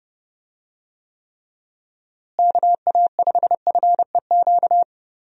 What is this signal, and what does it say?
Morse code, heard as a keyed tone, about 30 wpm, 700 Hz. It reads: KA5FEQ
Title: USA Call Signs - 6 Characters - Repeated 3x - 30wpm